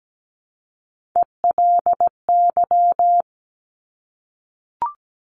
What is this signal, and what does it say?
Morse code, heard as a keyed tone, about 17 wpm, 700 Hz. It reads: ELY
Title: Top Three Letter Combinations in Words 17wpm